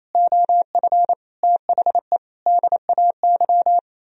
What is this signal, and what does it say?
Morse code, heard as a keyed tone, about 34 wpm, 700 Hz. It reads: OF THE DAY
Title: Speed-Racing - Sets of 5 Words 28wpm